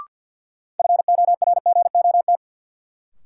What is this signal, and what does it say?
Morse code, heard as a keyed tone, about 53 wpm, 700 Hz. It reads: FORGOT
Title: Sentences from Top 1400 Words - New Word Drill - 50wpm